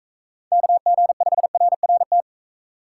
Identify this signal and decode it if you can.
Morse code, heard as a keyed tone, about 42 wpm, 700 Hz. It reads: KG5RRT